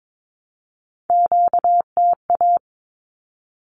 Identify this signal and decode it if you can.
Morse code, heard as a keyed tone, about 22 wpm, 700 Hz. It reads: QTA
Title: Q-Codes 22wpm